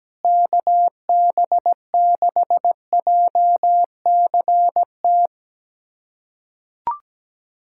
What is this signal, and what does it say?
Morse code, heard as a keyed tone, about 17 wpm, 700 Hz. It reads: KB6JCT
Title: USA Callsign DE Callsign - 6 Characters - 17wpm